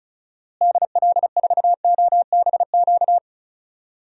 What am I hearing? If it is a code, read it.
Morse code, heard as a keyed tone, about 36 wpm, 700 Hz. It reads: DL4OBQ
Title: USA Call Signs - 6 Characters - Repeated 3x - 35wpm